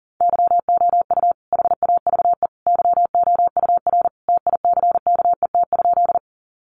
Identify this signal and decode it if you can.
Morse code, heard as a keyed tone, about 40 wpm, 700 Hz. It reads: YOU HAVE YOUR TICKET?
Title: Sentences from Top 1400 Words 40wpm